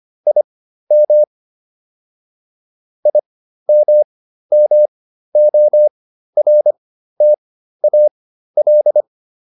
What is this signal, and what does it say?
Morse code, heard as a keyed tone, about 25 wpm, spaced as for 13 wpm, 600 Hz. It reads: IM IMMORTAL